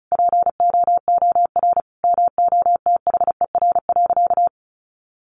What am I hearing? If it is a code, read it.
Morse code, heard as a keyed tone, about 34 wpm, 700 Hz. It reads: POOR MOTHER.
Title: Sentences from Top 1000 Words 35wpm